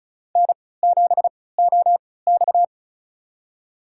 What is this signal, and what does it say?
Morse code, heard as a keyed tone, about 36 wpm, spaced as for 20 wpm, 690 Hz. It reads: N7OX
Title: USA Call Signs - 4 Characters - 3x Character Spacing - 35wpm